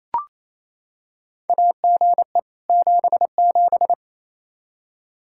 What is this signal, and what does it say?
Morse code, heard as a keyed tone, about 28 wpm, 700 Hz. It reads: AGE 77
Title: Practice QSO - Exchange 3 - More Info - 28wpm